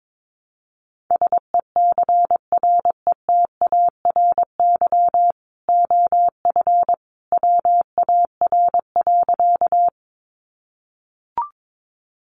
Morse code, heard as a keyed tone, about 22 wpm, 700 Hz. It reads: SECRETARY OF WAR.
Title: Sentences from Top 1300 Words 22wpm